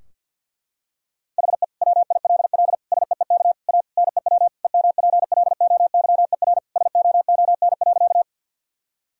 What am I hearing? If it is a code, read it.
Morse code, heard as a keyed tone, about 50 wpm, 700 Hz. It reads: HE WILL SEEK A NEW EMPLOYER SOON.
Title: Sentences from Top 1600 Words 50wpm